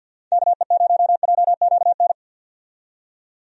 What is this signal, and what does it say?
Morse code, heard as a keyed tone, about 46 wpm, 700 Hz. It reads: KE0JQN
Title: USA Call Signs - 6 Characters - 50wpm